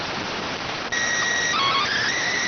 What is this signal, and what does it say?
SSTV (slow-scan television), PD 290.